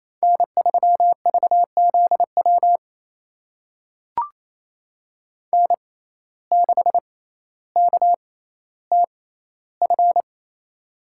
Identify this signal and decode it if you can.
Morse code, heard as a keyed tone, about 28 wpm, spaced as for 14 wpm, 700 Hz. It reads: N3VZW N6KTF
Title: USA Call Signs - 5 Characters - 6x Character Spacing - 28wpm